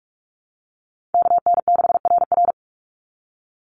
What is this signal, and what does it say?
Morse code, heard as a keyed tone, about 44 wpm, 700 Hz. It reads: KN6RR
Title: USA Call Signs 45wpm